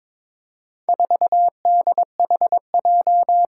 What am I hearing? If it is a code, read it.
Morse code, heard as a keyed tone, about 22 wpm, 700 Hz. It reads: 4DHJ
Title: Four Challenging Letter-Number Combinations 22wpm